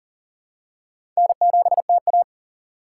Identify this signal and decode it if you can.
Morse code, heard as a keyed tone, about 38 wpm, 700 Hz. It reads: N7TA
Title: USA Call Signs - 4 Characters - 4x Character Spacing - 40wpm